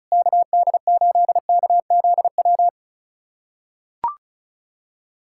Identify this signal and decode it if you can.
Morse code, heard as a keyed tone, about 36 wpm, 700 Hz. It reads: KD8KZW
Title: USA Callsign DE Callsign 35wpm